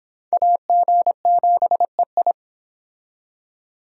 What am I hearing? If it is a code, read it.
Morse code, heard as a keyed tone, about 25 wpm, 700 Hz. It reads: AG7EI